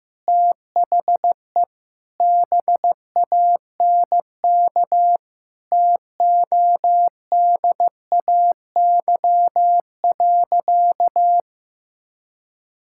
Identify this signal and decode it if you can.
Morse code, heard as a keyed tone, about 15 wpm, 700 Hz. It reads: THE BANK TODAY.
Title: Sentences from Top 1000 Words - Review Part 6 - 15wpm